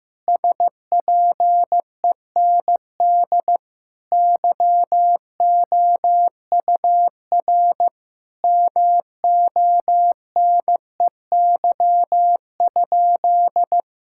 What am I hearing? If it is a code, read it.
Morse code, heard as a keyed tone, about 15 wpm, 700 Hz. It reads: SPEND YOUR MONEY?
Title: Sentences from Top 1000 Words - Review Part 5 - 15wpm